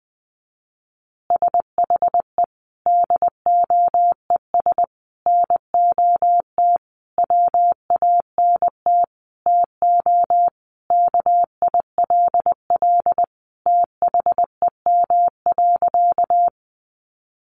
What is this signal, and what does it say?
Morse code, heard as a keyed tone, about 20 wpm, 700 Hz. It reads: SHE DOES NOT WANT TO KILL THEM.
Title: Additional Easy Phrases 20wpm